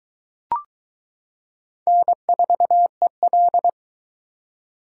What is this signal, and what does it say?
Morse code, heard as a keyed tone, about 22 wpm, 700 Hz. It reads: N4EL